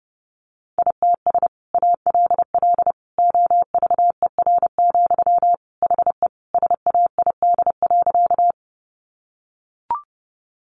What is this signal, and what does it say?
Morse code, heard as a keyed tone, about 30 wpm, 700 Hz. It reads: ITS ALL OVER, HE SAID.
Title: Sentences from Top 600 Words 30wpm